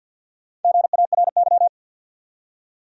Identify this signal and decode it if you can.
Morse code, heard as a keyed tone, about 53 wpm, 700 Hz. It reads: GARY
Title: USA Common Names - Max 5 Letters - 50wpm